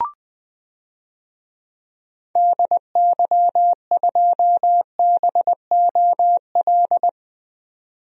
Morse code, heard as a keyed tone, about 20 wpm, 700 Hz. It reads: DY2BOL
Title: DXCC-100 Call Signs - 6 Characters - 20wpm